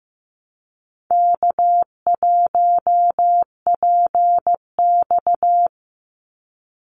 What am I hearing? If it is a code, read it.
Morse code, heard as a keyed tone, about 15 wpm, 700 Hz. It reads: K1PX
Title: Easy Call Signs Repeated 3x - 15wpm